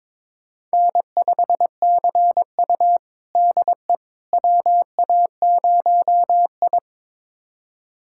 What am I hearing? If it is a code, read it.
Morse code, heard as a keyed tone, about 22 wpm, 700 Hz. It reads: N5CU DE WA0I
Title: USA Callsign DE Callsign - 4 Characters - 22wpm